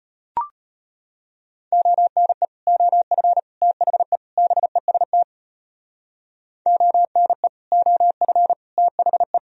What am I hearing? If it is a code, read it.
Morse code, heard as a keyed tone, about 31 wpm, 700 Hz. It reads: ONE OF THE BEST ONE OF THE